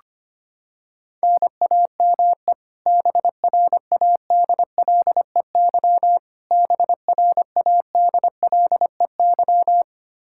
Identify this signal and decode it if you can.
Morse code, heard as a keyed tone, about 25 wpm, 700 Hz. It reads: NAME BRADLEY BRADLEY
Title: Practice QSO - Exchange 1 - Beginning - 25wpm